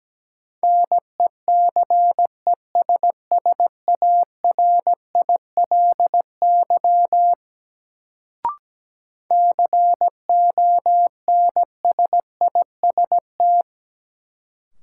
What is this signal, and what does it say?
Morse code, heard as a keyed tone, about 17 wpm, 700 Hz. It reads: NECESSARILY CONSIST
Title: Sentences from Top 1400 Words - New Word Drill Alt - 17wpm